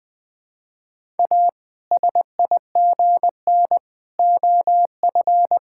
Morse code, heard as a keyed tone, about 20 wpm, 700 Hz. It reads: A SIGN OF
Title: Sets of 6 Words - 20wpm